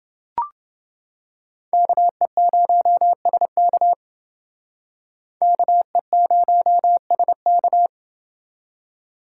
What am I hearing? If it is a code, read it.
Morse code, heard as a keyed tone, about 30 wpm, 700 Hz. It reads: KE0SK KE0SK